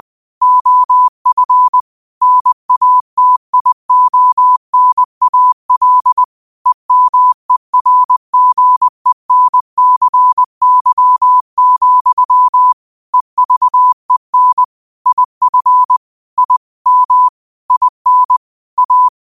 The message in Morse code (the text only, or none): OF NATIONAL EMERGENCY, EVEN IF I M IN A